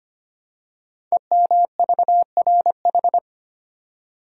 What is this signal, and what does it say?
Morse code, heard as a keyed tone, about 25 wpm, 700 Hz. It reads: EMVRH
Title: Code Groups - 5 Letters - 25wpm